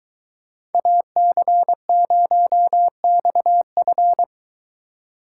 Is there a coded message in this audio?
Yes: Morse code from 0.7 to 4.2 s, heard as a keyed tone, about 23 wpm, 700 Hz. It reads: AC0XF